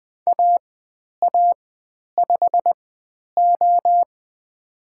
Morse code, heard as a keyed tone, about 20 wpm, spaced as for 10 wpm, 700 Hz. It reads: AA5O
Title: 095 Lesson 16g - Letter c - Call Signs 20wpm